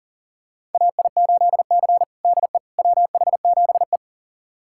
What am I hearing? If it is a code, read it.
Morse code, heard as a keyed tone, about 40 wpm, 700 Hz. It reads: AI8C DE WH7E
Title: USA Callsign DE Callsign - 4 Characters - 40wpm